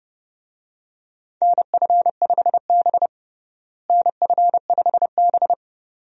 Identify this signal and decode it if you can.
Morse code, heard as a keyed tone, about 30 wpm, 700 Hz. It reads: NF5B NF5B
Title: USA Call Signs - 4 Characters - Repeated 2x - 30wpm